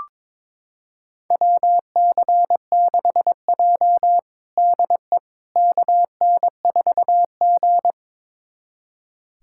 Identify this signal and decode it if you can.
Morse code, heard as a keyed tone, about 23 wpm, 700 Hz. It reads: WC6J DE KN4G
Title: USA Callsign DE Callsign - 4 Characters - 22wpm